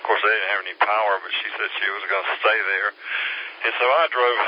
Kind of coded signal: Emergency Alert System (SAME) tones